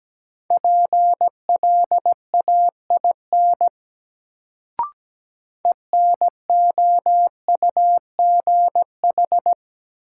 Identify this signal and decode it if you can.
Morse code, heard as a keyed tone, about 17 wpm, 700 Hz. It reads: PLAIN ENOUGH
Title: Sentences from Top 400 Words - New Word Drill - 17wpm